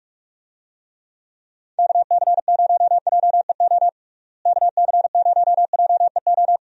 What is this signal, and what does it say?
Morse code, heard as a keyed tone, about 46 wpm, 700 Hz. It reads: KC0JEO KC0JEO
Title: USA Call Signs - 6 Characters - Repeated 3x - 45wpm